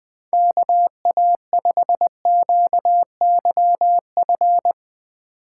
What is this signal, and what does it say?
Morse code, heard as a keyed tone, about 20 wpm, 700 Hz. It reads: KA5QYF
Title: USA Call Signs - 6 Characters - 3x Character Spacing - 20wpm